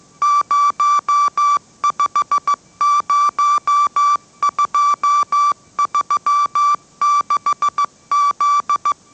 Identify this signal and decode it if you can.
Morse code, heard as a keyed tone, about 15 wpm, 1160 Hz. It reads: 050236Z
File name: mrx / mrx065 16wpm